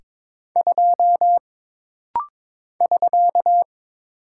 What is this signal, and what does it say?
Morse code, heard as a keyed tone, about 22 wpm, 700 Hz. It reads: 2 <SK>
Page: This rapid-fire timing is geared to push your ICR (Instant Character Recognition) skills.